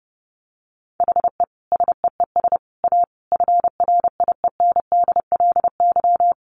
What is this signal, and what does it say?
Morse code, heard as a keyed tone, about 28 wpm, 700 Hz. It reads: HE SEES A FRIENDLY
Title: Easy Phrases 30wpm